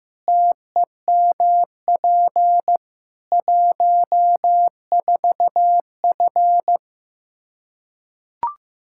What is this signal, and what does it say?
Morse code, heard as a keyed tone, about 15 wpm, 700 Hz. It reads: TEMP 14F
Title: K1USN SST QSO Practice 15wpm